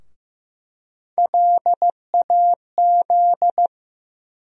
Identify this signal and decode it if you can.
Morse code, heard as a keyed tone, about 15 wpm, 700 Hz. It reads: LAZ
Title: USA Call Sign Postfixes - 3 Letters 15wpm